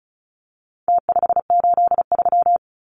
Morse code, heard as a keyed tone, about 34 wpm, 700 Hz. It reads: T583
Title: Numbers - Contest Style - 4-Digit 35wpm